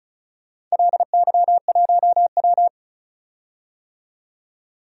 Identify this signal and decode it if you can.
Morse code, heard as a keyed tone, about 35 wpm, 700 Hz. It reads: LY1W
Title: DXCC-100 Call Signs - 4 Characters - 35wpm